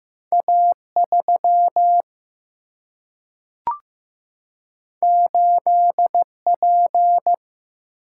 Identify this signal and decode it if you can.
Morse code, heard as a keyed tone, about 15 wpm, 700 Hz. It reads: A3 8P
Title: Single-Letter Single-Number 15wpm